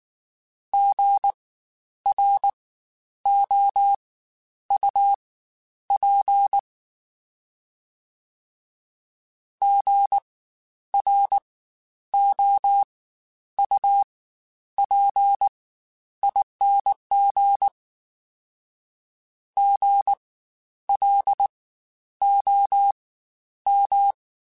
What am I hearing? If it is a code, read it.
Morse code, heard as a keyed tone, about 19 wpm, spaced as for 10 wpm, 800 Hz. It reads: GROUP GROUPING GLOM
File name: Spaced Alphas / Spaced Alphas mp3s / Gs sp @18